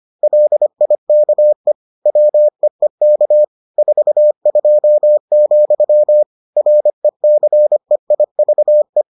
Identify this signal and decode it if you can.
Morse code, heard as a keyed tone, about 25 wpm, 600 Hz. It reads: LIKE WEEK 42, RECEIVE